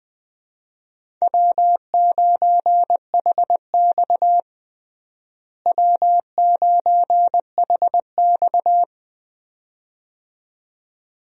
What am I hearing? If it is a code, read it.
Morse code, heard as a keyed tone, about 20 wpm, 700 Hz. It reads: W9HX W9HX
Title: USA Call Signs - 4 Characters - Repeated 2x - 20wpm